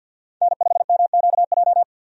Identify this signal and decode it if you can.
Morse code, heard as a keyed tone, about 51 wpm, 700 Hz. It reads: N5MQJ
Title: USA Call Signs - 5 Characters - 50wpm